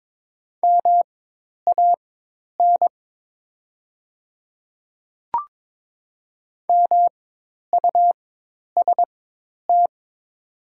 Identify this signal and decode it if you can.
Morse code, heard as a keyed tone, about 22 wpm, spaced as for 10 wpm, 700 Hz. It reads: MAN MUST
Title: 4x Character Spacing - Top 200 Words - 22wpm